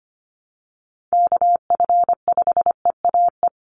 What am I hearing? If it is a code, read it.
Morse code, heard as a keyed tone, about 25 wpm, 700 Hz. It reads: KF5EAE
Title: USA Call Signs 25wpm